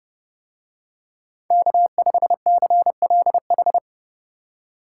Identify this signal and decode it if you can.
Morse code, heard as a keyed tone, about 30 wpm, 700 Hz. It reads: K5CLH